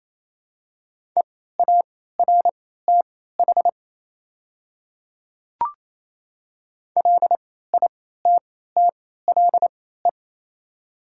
Morse code, heard as a keyed tone, about 27 wpm, spaced as for 16 wpm, 690 Hz. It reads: EARTH LITTLE
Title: 3x Character Spacing - Top 200 Words - 28wpm